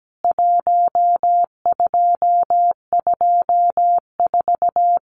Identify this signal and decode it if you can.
Morse code, heard as a keyed tone, about 17 wpm, 700 Hz. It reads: 1224
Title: CWT Exchange 17wpm